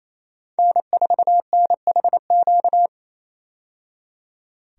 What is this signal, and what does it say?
Morse code, heard as a keyed tone, about 28 wpm, 700 Hz. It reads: N4NHQ
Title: USA Call Signs - 5 Characters - 28wpm